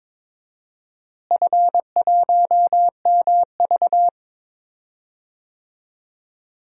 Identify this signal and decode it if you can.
Morse code, heard as a keyed tone, about 22 wpm, 700 Hz. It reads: F1MV
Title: DXCC-100 Call Signs - 4 Characters - Repeated 2x - 22wpm